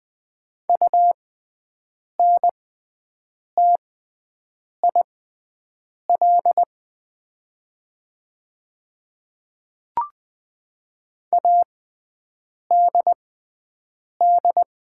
Morse code, heard as a keyed tone, about 20 wpm, spaced as for 7 wpm, 700 Hz. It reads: UNTIL ADD
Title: 6x Character Spacing - Top 300 Words - 20wpm